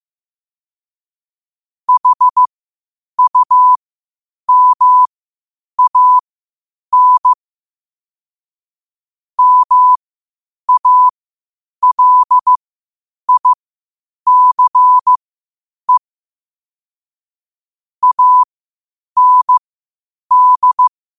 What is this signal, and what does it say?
Morse code, heard as a keyed tone, about 15 wpm, spaced as for 8 wpm, 1000 Hz. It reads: HUMAN MALICE AND
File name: qotdfiles / 8wpm